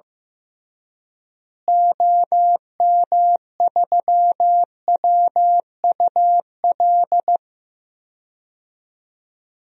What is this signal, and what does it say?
Morse code, heard as a keyed tone, about 15 wpm, 700 Hz. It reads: OM3WUL
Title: DXCC-100 Call Signs - 6 Characters - Repeated 3x - 15wpm